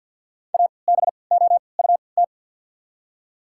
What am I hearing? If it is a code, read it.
Morse code, heard as a keyed tone, about 48 wpm, spaced as for 28 wpm, 700 Hz. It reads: ABOUT
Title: 3x Character Spacing - Top 100 Words - 50wpm